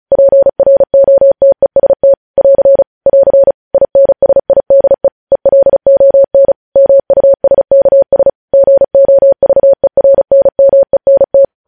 Morse code, heard as a keyed tone, about 36 wpm, 550 Hz. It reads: PROTEST <AR> <AR> INSIDE ELON MUSKS GOVERNMENT
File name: podcast / News headlines-1x-35x35